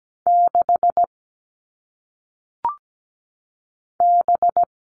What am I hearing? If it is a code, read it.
Morse code, heard as a keyed tone, about 17 wpm, 700 Hz. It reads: 6 B
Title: All Cut Numbers - 1-Digit 17wpm